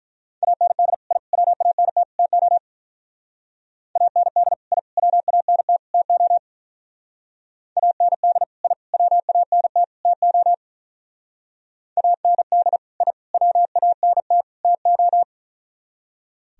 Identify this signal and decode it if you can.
Morse code, heard as a keyed tone, about 43 wpm, 700 Hz. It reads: AND I WANT TO AND I WANT TO AND I WANT TO AND I WANT TO
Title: Speed-Racing - Sets of 4 Words 35wpm